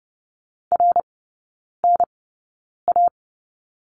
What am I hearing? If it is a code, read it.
Morse code, heard as a keyed tone, about 30 wpm, spaced as for 9 wpm, 700 Hz. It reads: RNA
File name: 081 Lesson 14e - Number 2 - Three Characters 30wpm